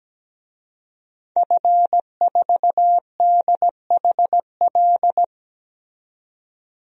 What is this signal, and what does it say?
Morse code, heard as a keyed tone, about 17 wpm, 700 Hz. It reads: F4DHL
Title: DXCC-100 Call Signs - 5 Characters - 17wpm